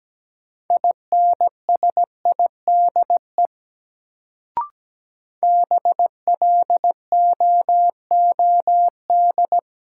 Morse code, heard as a keyed tone, about 17 wpm, 700 Hz. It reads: INSIDE BLOOD
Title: Sentences from Top 1000 Words - Review Part 2 - New Word Drill - 17wpm